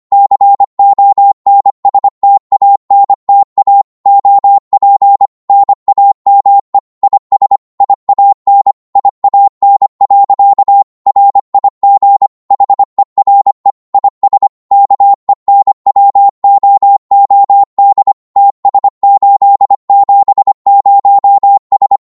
Morse code, heard as a keyed tone, about 25 wpm, 800 Hz. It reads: CONSTANTA OP NAME IS IAN IAN. RIG HERE IS KENWOOD TS870S
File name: Morse / qso-25wpm